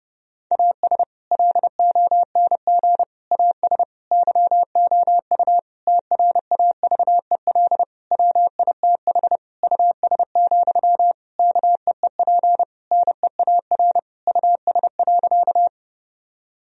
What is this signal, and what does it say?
Morse code, heard as a keyed tone, about 30 wpm, 700 Hz. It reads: AS LONG AS YOU TRAVEL WITH US, KEEP NEAR US.
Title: Sentences from Top 1000 Words - Review Part 2 - 30wpm